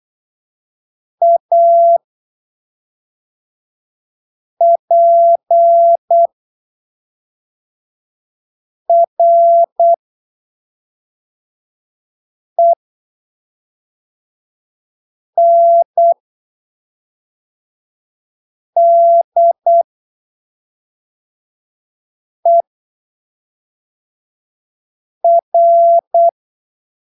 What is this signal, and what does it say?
Morse code, heard as a keyed tone, about 8 wpm, spaced as for 3 wpm, 680 Hz. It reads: APRENDER
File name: arquivos / texto / texto 21 08ppm